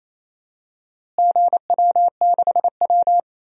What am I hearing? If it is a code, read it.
Morse code, heard as a keyed tone, about 28 wpm, 700 Hz. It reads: GW6W
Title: DXCC-100 Call Signs - 4 Characters - Repeated 2x - 28wpm